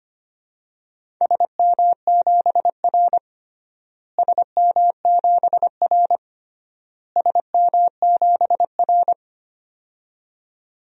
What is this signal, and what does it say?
Morse code, heard as a keyed tone, about 25 wpm, 700 Hz. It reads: SM7R SM7R SM7R
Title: DXCC-100 Call Signs - 4 Characters - Repeated 3x - 25wpm